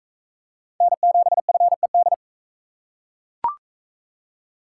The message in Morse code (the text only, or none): N7FED